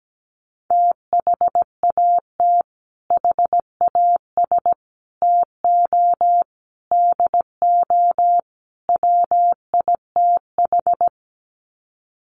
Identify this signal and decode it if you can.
Morse code, heard as a keyed tone, about 17 wpm, 700 Hz. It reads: THAT HAS TO DO WITH
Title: Sets of 5 Words Encore2 17wpm